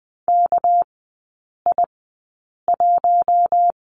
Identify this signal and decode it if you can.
Morse code, heard as a keyed tone, about 20 wpm, spaced as for 8 wpm, 700 Hz. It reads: KI1
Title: 198 Lesson 32e - Letter k - Three Characters 20wpm